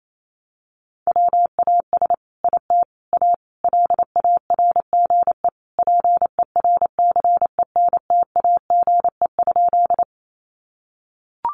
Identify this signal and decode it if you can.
Morse code, heard as a keyed tone, about 28 wpm, 700 Hz. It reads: WAS IT A LARGE PERCENTAGE?
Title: Sentences from Top 1300 Words 28wpm